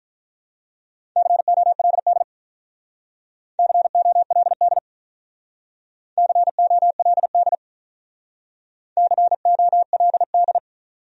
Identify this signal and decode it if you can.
Morse code, heard as a keyed tone, about 44 wpm, 700 Hz. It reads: COLD COLD COLD COLD
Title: Speed-Racing Top 500 Words 35wpm